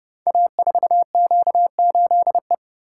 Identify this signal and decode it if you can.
Morse code, heard as a keyed tone, about 30 wpm, 700 Hz. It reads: A4Q8E
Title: Code Groups - 5 Letter:Numbers - 30wpm